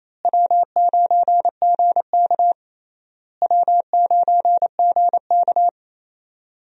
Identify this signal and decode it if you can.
Morse code, heard as a keyed tone, about 28 wpm, 700 Hz. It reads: W9GK W9GK
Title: USA Call Signs - 4 Characters - Repeated 3x - 28wpm